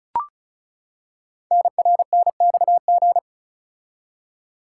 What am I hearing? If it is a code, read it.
Morse code, heard as a keyed tone, about 36 wpm, 690 Hz. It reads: NRNXG
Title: Code Groups - 5 Letters - 35wpm